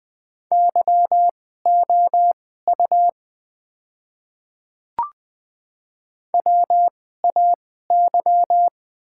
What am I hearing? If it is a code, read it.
Morse code, heard as a keyed tone, about 20 wpm, spaced as for 14 wpm, 690 Hz. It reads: YOU WAY
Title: 2x Character Spacing - Top 100 Words - 20wpm